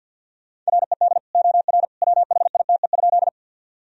Morse code, heard as a keyed tone, about 50 wpm, 700 Hz. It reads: RED OR WHITE?
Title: Sentences from Top 1000 Words - Review Part 2 - 50wpm